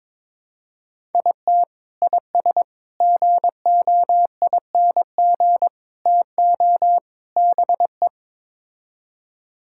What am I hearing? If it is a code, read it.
Morse code, heard as a keyed tone, about 22 wpm, 700 Hz. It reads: IT IS GOING TO BE
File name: practice / 22wpm / Build A Phrase - 5-Word Phrases - 22wpm